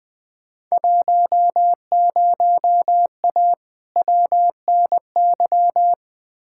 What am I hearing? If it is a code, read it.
Morse code, heard as a keyed tone, about 20 wpm, 700 Hz. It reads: 10A WNY
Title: Field Day Exchanges - 20wpm